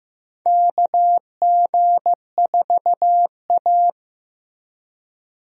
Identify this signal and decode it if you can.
Morse code, heard as a keyed tone, about 15 wpm, 700 Hz. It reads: KG4A
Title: USA Call Signs - 4 Characters - Repeated 2x - 15wpm